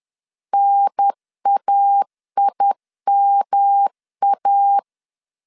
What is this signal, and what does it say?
Morse code, heard as a keyed tone, about 10 wpm, 800 Hz. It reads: NAIMA
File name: CW / DET / 1 3 1500C S C 10wpm 3335